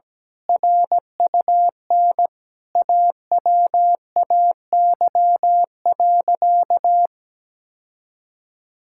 Morse code, heard as a keyed tone, about 17 wpm, 700 Hz. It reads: RUN AWAY.
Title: Sentences from Top 1600 Words 17wpm